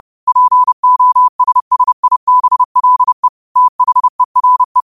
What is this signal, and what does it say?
Morse code, heard as a keyed tone, about 30 wpm, 1000 Hz. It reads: POSSIBLE THERE